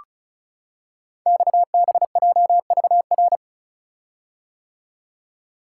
Morse code, heard as a keyed tone, about 36 wpm, 700 Hz. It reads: XBJVR
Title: Code Groups - 5 Letters - 35wpm